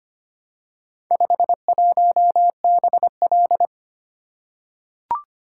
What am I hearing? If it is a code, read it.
Morse code, heard as a keyed tone, about 25 wpm, 700 Hz. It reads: 51BL
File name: practice / 25wpm / Four Letter-Number Combinations - 25wpm